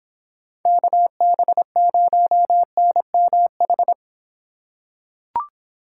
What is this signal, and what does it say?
Morse code, heard as a keyed tone, about 26 wpm, 700 Hz. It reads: KB0NMH